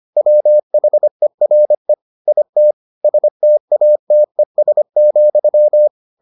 Morse code, heard as a keyed tone, about 25 wpm, 600 Hz. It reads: WHERE IT STATES,